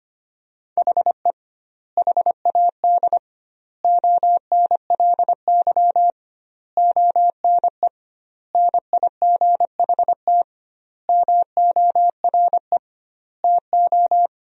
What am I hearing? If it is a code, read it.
Morse code, heard as a keyed tone, about 25 wpm, 700 Hz. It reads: HE HAD ONLY ONE NIGHT MORE TO
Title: 2x Word-Spacing - Sentences from Top 300 Words - 25wpm